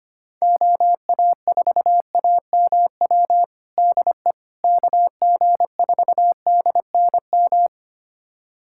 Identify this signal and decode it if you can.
Morse code, heard as a keyed tone, about 25 wpm, 700 Hz. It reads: OA4AMW DE KG4DNM
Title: USA Callsign DE Callsign 25wpm